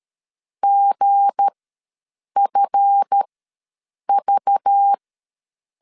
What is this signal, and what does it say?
Morse code, heard as a keyed tone, about 13 wpm, spaced as for 7 wpm, 800 Hz. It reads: GFV
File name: LEZIONI / 7 GF9Z / 7 GF ws 12wpm 3140